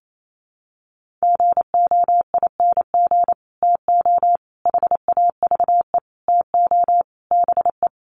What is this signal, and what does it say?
Morse code, heard as a keyed tone, about 28 wpm, 700 Hz. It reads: GOING TO HAVE TO BE
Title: Sets of 5 Words 28wpm